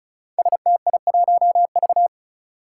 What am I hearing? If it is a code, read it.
Morse code, heard as a keyed tone, about 35 wpm, 700 Hz. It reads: STI1V
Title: Five Challenging Letter-Number Combinations 35wpm